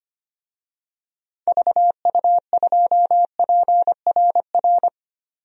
Morse code, heard as a keyed tone, about 25 wpm, 700 Hz. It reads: VU2PRR
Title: DXCC-100 Call Signs - 6 Characters - 25wpm